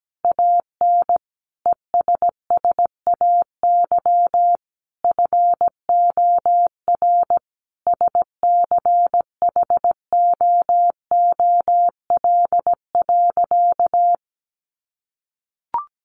Morse code, heard as a keyed tone, about 17 wpm, 700 Hz. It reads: AN ESSAY FOR SCHOOL.
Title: Sentences from Top 1500 Words 17wpm